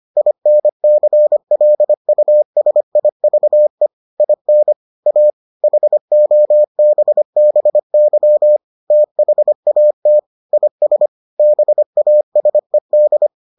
Morse code, heard as a keyed tone, about 25 wpm, 600 Hz. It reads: INCLUSIVE IN A HOBBY THAT IS BASED